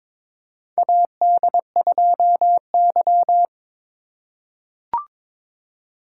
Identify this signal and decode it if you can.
Morse code, heard as a keyed tone, about 22 wpm, 700 Hz. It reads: AD2Y
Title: USA Call Signs - 2x Character Spacing - 22wpm